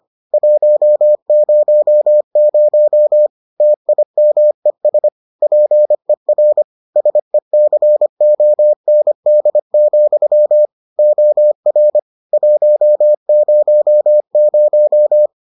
Morse code, heard as a keyed tone, about 25 wpm, 600 Hz. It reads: 100 TIMES PER SECOND, OR 100